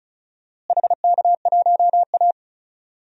Morse code, heard as a keyed tone, about 34 wpm, 690 Hz. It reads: HK1A
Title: DXCC-100 Call Signs - 4 Characters - Repeated 2x - 35wpm